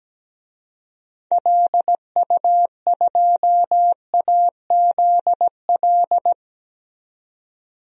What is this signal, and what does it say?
Morse code, heard as a keyed tone, about 17 wpm, 700 Hz. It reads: LU2AZL
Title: DXCC-100 Call Signs - 6 Characters - 17wpm